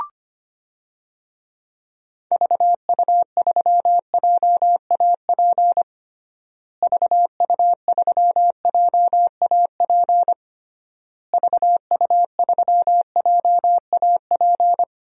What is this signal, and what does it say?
Morse code, heard as a keyed tone, about 25 wpm, 700 Hz. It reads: VU3JAP VU3JAP VU3JAP
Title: DXCC-100 Call Signs - 6 Characters - Repeated 3x - 25wpm